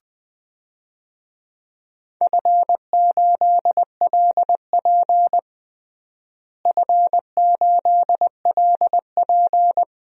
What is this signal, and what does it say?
Morse code, heard as a keyed tone, about 20 wpm, 700 Hz. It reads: F8LP F8LP
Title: DXCC-100 Call Signs - 4 Characters - Repeated 3x - 20wpm